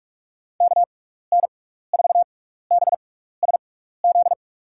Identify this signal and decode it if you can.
Morse code, heard as a keyed tone, about 46 wpm, spaced as for 16 wpm, 700 Hz. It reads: KN4BSZ
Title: USA Call Signs - 6 Characters - 6x Character Spacing - 45wpm